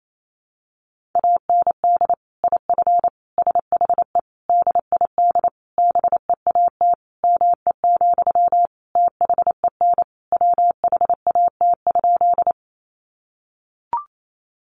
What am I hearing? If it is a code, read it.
Morse code, heard as a keyed tone, about 28 wpm, 700 Hz. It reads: AND IF SHE DID BEAT ME, THEN WHAT?
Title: Sentences from Top 1300 Words 28wpm